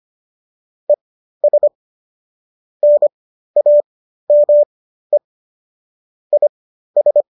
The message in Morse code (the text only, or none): ES NAME IS